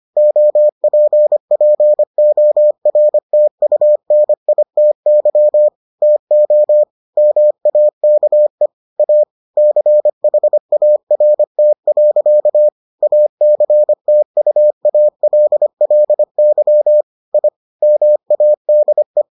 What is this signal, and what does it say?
Morse code, heard as a keyed tone, about 25 wpm, 600 Hz. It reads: OPPORTUNITY TO MAKE A CHART. ACTUALLY I MADE